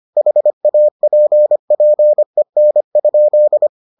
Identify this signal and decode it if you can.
Morse code, heard as a keyed tone, about 25 wpm, 600 Hz. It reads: HAPPEN?